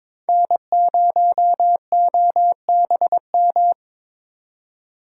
Morse code, heard as a keyed tone, about 26 wpm, 700 Hz. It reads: N0OBM
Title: Speed Racing - USA 1x3 Call Signs 22wpm